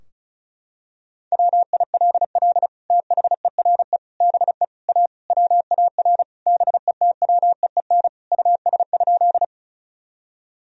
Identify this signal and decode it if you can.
Morse code, heard as a keyed tone, about 34 wpm, 700 Hz. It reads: WILL THERE BE A WAR BETWEEN US?
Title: Sentences from Top 1000 Words - Review Part 2 - 35wpm